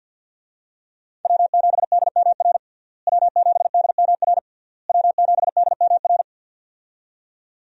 Morse code, heard as a keyed tone, about 52 wpm, 700 Hz. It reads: W7DMR W7DMR W7DMR
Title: USA Call Signs - 5 Characters - Repeated 3x - 50wpm